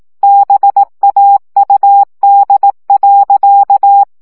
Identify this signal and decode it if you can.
Morse code, heard as a keyed tone, about 18 wpm, 800 Hz. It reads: BAUD.
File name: CW 15 baud